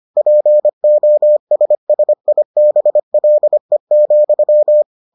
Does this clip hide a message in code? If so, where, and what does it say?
Yes: Morse code from 0.2 to 4.8 s, heard as a keyed tone, about 25 wpm, 600 Hz. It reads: POSSIBLE,